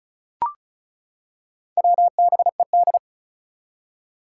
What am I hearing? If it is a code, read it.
Morse code, heard as a keyed tone, about 36 wpm, 690 Hz. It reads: WBED
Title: Four Challenging Letter-Number Combinations 35wpm